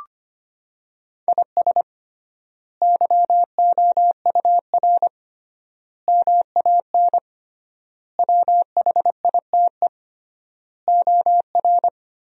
Morse code, heard as a keyed tone, about 25 wpm, 700 Hz. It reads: IS YOUR MAN WHITE OR
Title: 3x Word-Spacing - Sentences from Top 400 Words - 25wpm